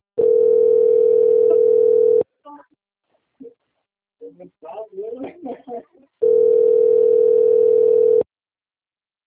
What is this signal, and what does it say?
Ringback tone (North America), 440 + 480 Hz, with speech.
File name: out- - - - - .56027